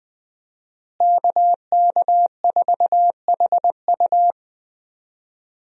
Morse code, heard as a keyed tone, about 20 wpm, 700 Hz. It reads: KK4HU
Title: USA Call Signs - 5 Characters - Repeated 3x - 20wpm